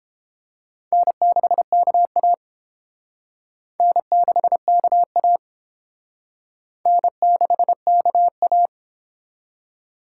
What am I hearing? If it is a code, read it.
Morse code, heard as a keyed tone, about 28 wpm, 700 Hz. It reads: N6KA N6KA N6KA